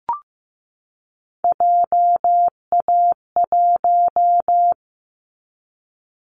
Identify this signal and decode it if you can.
Morse code, heard as a keyed tone, about 15 wpm, 700 Hz. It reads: JA1
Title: Callsign Prefixes for Top 50 DXCC - 15wpm